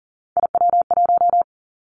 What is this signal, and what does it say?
Morse code, heard as a keyed tone, about 40 wpm, 700 Hz. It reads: IW1
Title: Callsign Prefixes for Top 50 DXCC - 40wpm